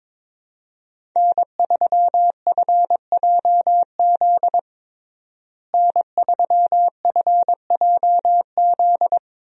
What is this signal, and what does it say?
Morse code, heard as a keyed tone, about 22 wpm, 700 Hz. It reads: N3FJZ N3FJZ
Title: USA Call Signs - 5 Characters - Repeated 2x - 22wpm